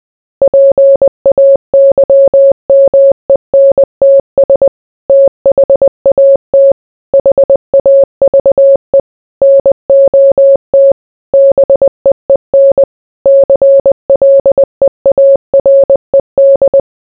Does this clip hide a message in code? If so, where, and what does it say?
Yes: Morse code from 0.4 to 16.8 s, heard as a keyed tone, about 20 wpm, 550 Hz. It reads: PAYMENTS THAT HAVE NOT BEEN CLEARED